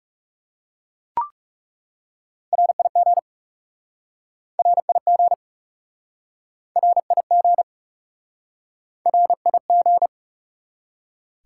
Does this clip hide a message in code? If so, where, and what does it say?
Yes: Morse code from 2.5 to 10.1 s, heard as a keyed tone, about 38 wpm, 700 Hz. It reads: RIG RIG RIG RIG